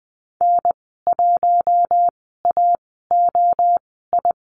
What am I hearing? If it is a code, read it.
Morse code, heard as a keyed tone, about 20 wpm, spaced as for 15 wpm, 700 Hz. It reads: N1AOI
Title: 055 Lesson 10g - Letter r - Call Signs 20wpm